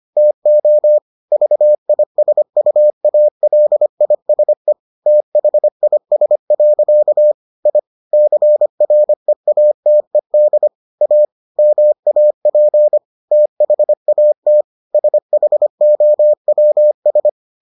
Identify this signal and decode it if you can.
Morse code, heard as a keyed tone, about 25 wpm, 600 Hz. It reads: TO VISUALISE THIS. I CREATED A MAP THAT SHOWS